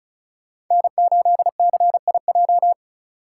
Morse code, heard as a keyed tone, about 42 wpm, 700 Hz. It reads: N8CIJ
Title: Speed Racing - USA 1x3 Call Signs 35wpm